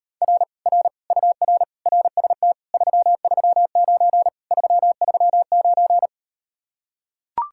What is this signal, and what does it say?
Morse code, heard as a keyed tone, about 38 wpm, 700 Hz. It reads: R R UR RST 339 339